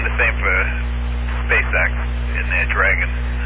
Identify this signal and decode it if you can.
Emergency Alert System (SAME) tones, heard as harsh buzzing bursts, with speech. It is heard from 0.4 to 3.1 s.